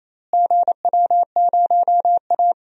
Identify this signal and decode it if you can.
Morse code, heard as a keyed tone, about 28 wpm, 700 Hz. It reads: GW0A
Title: DXCC-100 Call Signs - 4 Characters - Repeated 2x - 28wpm